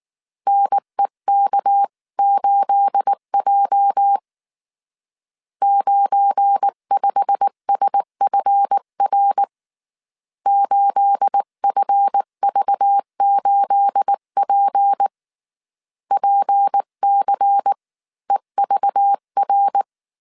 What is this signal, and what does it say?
Morse code, heard as a keyed tone, about 19 wpm, 800 Hz. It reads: NEK8J 95SFR 8FV8P PCEVR
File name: CW / DET / 1 9 1500C S B 18wpm 2551